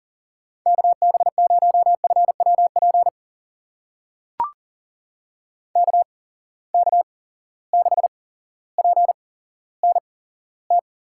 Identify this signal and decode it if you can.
Morse code, heard as a keyed tone, about 40 wpm, spaced as for 17 wpm, 700 Hz. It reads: KB0FWP KK6PNT
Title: USA Call Signs - 6 Characters - 8x Character Spacing - 40wpm